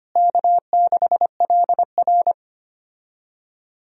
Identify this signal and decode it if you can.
Morse code, heard as a keyed tone, about 25 wpm, 700 Hz. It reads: K6LR
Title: USA Call Signs - 4 Characters - Repeated 3x - 25wpm